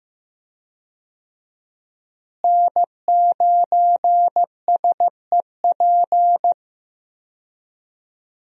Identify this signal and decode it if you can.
Morse code, heard as a keyed tone, about 15 wpm, 700 Hz. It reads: N9SEP
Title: USA Call Signs - 5 Characters - Repeated 2x - 15wpm